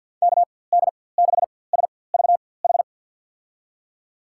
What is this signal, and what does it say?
Morse code, heard as a keyed tone, about 51 wpm, spaced as for 23 wpm, 700 Hz. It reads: KD6SVH
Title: USA Call Signs - 6 Characters - 4x Character Spacing - 50wpm